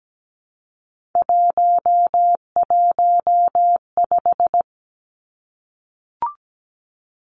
Morse code, heard as a keyed tone, about 17 wpm, 700 Hz. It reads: 115
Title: Three Challenging Letter-Number Combinations 17wpm